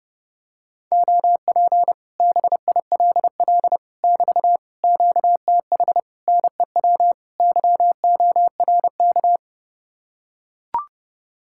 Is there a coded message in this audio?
Yes: Morse code from 0.9 to 9.4 s, heard as a keyed tone, about 32 wpm, 700 Hz. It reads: OP BILL <BT> QTH NEW YORK